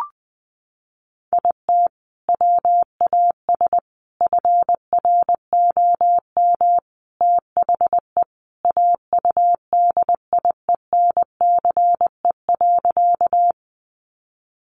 Morse code, heard as a keyed tone, about 20 wpm, 700 Hz. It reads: IT WAS FROM THE AUDIENCE.
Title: Sentences from Top 800 Words 20wpm